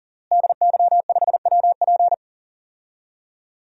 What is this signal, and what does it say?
Morse code, heard as a keyed tone, about 40 wpm, 700 Hz. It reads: DY5WP
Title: DXCC-100 Call Signs - 5 Characters - Repeated 3x - 40wpm